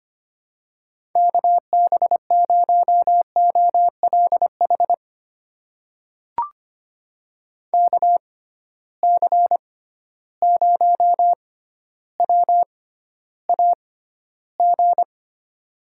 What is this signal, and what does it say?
Morse code, heard as a keyed tone, about 25 wpm, spaced as for 13 wpm, 700 Hz. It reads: KB0OLH KC0WAG
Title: USA Call Signs - 6x Character Spacing - 25wpm